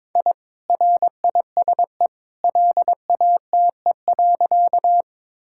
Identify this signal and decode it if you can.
Morse code, heard as a keyed tone, about 22 wpm, 700 Hz. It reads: I RISE LATE.
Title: Sentences from Top 1000 Words - Review Part 6 - 22wpm